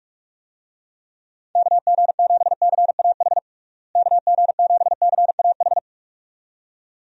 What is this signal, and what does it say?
Morse code, heard as a keyed tone, about 46 wpm, 700 Hz. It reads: KG7CAH KG7CAH
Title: USA Call Signs - 6 Characters - Repeated 2x - 45wpm